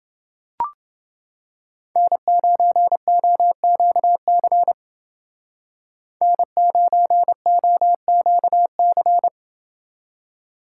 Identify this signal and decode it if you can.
Morse code, heard as a keyed tone, about 29 wpm, 700 Hz. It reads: N9OQC N9OQC